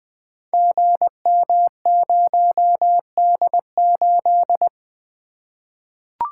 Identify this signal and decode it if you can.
Morse code, heard as a keyed tone, about 20 wpm, 700 Hz. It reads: GM0D8
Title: Code Groups - 5 Letter:Numbers - 20wpm